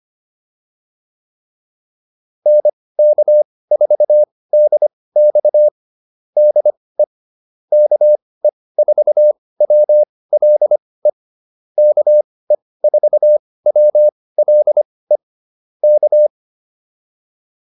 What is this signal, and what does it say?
Morse code, heard as a keyed tone, about 25 wpm, spaced as for 18 wpm, 600 Hz. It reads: NK4DX DE KE4WLE KE4WLE K
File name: QSO110 18